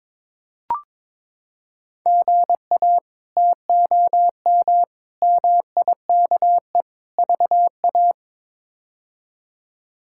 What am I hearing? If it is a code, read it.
Morse code, heard as a keyed tone, about 21 wpm, 700 Hz. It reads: GA TOM MIKE VA
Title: K1USN SST QSO Practice - 22wpm